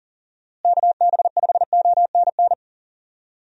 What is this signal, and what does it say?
Morse code, heard as a keyed tone, about 40 wpm, 700 Hz. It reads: KB5ONN
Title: USA Call Signs - 6 Characters - Repeated 3x - 40wpm